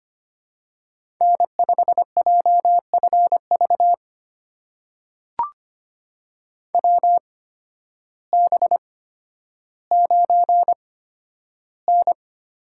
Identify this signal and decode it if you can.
Morse code, heard as a keyed tone, about 25 wpm, spaced as for 12 wpm, 700 Hz. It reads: N5JFV WB9N
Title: USA Call Signs - 8x Character Spacing - 25wpm